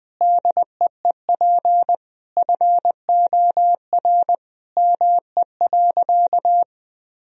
Morse code, heard as a keyed tone, about 20 wpm, 700 Hz. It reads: DEEP FOR ME.
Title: Sentences from Top 1000 Words - Review Part 3 - 20wpm